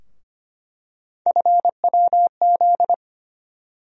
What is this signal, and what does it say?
Morse code, heard as a keyed tone, about 25 wpm, 700 Hz. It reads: FWZ
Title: USA Call Sign Postfixes - 3 Letters 25wpm